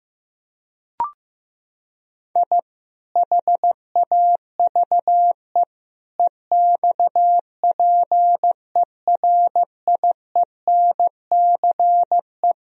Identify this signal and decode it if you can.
Morse code, heard as a keyed tone, about 15 wpm, 700 Hz. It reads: I HAVE EXPERIENCE
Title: Sentences from Top 1000 Words - Review Part 3 - 15wpm